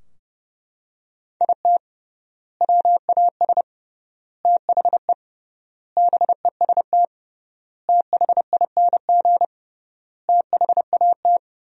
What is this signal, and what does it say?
Morse code, heard as a keyed tone, about 30 wpm, 700 Hz. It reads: IT WAS THE BEST THING THAT
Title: 3x Word-Spacing - Sentences from Top 400 Words - 30wpm